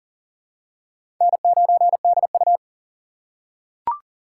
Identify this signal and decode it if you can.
Morse code, heard as a keyed tone, about 40 wpm, 700 Hz. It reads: N9DU
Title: USA Call Signs - 4x Character Spacing - 40wpm